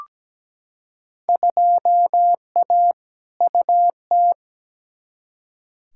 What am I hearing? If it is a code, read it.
Morse code, heard as a keyed tone, about 17 wpm, 700 Hz. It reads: 2A UT
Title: Field Day Exchanges - 17wpm